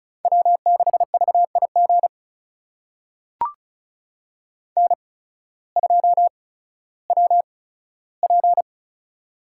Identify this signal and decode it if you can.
Morse code, heard as a keyed tone, about 36 wpm, spaced as for 16 wpm, 700 Hz. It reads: W6VIG N2WP
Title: USA Call Signs - 8x Character Spacing - 35wpm